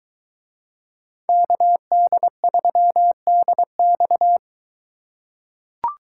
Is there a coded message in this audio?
Yes: Morse code from 1.3 to 4.4 s, heard as a keyed tone, about 22 wpm, 700 Hz. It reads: KD3DX